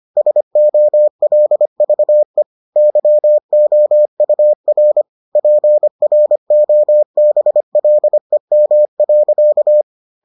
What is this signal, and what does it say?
Morse code, heard as a keyed tone, about 25 wpm, 600 Hz. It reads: SOLVE YOUR PROBLEM.